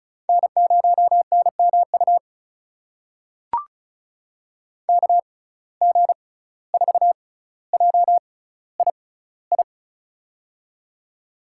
Morse code, heard as a keyed tone, about 34 wpm, spaced as for 17 wpm, 700 Hz. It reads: N0NMU KG4JII
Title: USA Call Signs - 6x Character Spacing - 35wpm